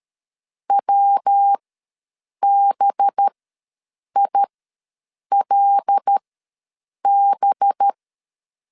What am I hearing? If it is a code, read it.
Morse code, heard as a keyed tone, about 13 wpm, spaced as for 7 wpm, 800 Hz. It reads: WBILB
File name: 5 LB ws 12wpm